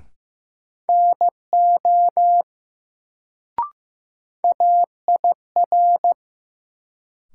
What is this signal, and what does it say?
Morse code, heard as a keyed tone, about 15 wpm, 700 Hz. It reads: NO AIR
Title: Sets of 2 Words - New Word Drill - 15wpm